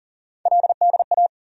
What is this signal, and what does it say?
Morse code, heard as a keyed tone, about 40 wpm, 700 Hz. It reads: LDA
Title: USA Call Sign Postfixes - 3 Letters 40wpm